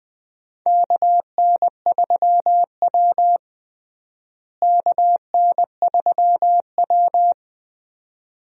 Morse code, heard as a keyed tone, about 20 wpm, 700 Hz. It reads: KN3W KN3W
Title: USA Call Signs - 4 Characters - Repeated 2x - 20wpm